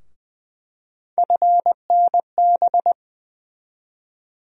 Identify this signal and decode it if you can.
Morse code, heard as a keyed tone, about 20 wpm, 700 Hz. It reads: FNB
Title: USA Call Sign Postfixes - 3 Letters 20wpm